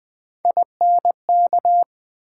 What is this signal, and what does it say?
Morse code, heard as a keyed tone, about 20 wpm, 700 Hz. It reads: INK
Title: Build A Word - 1 Letter At A Time - 3-Letter Words - 20wpm